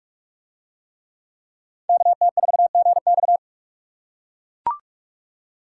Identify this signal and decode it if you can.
Morse code, heard as a keyed tone, about 43 wpm, 700 Hz. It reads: KT4GX